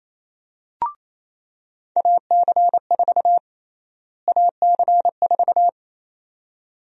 Morse code, heard as a keyed tone, about 27 wpm, 700 Hz. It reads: AC4 AC4
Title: USA Call Sign Prefixes - Repeated 2x - 28wpm